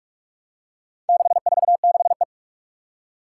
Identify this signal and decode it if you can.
Morse code, heard as a keyed tone, about 46 wpm, 700 Hz. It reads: 646E
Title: Four Challenging Letter-Number Combinations 45wpm